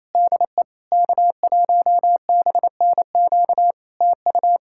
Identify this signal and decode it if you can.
Morse code, heard as a keyed tone, about 27 wpm, 700 Hz. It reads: DE K1BNQ TU
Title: K1USN SST QSO Practice 28wpm